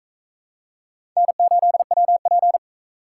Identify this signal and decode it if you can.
Morse code, heard as a keyed tone, about 42 wpm, 700 Hz. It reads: N8WP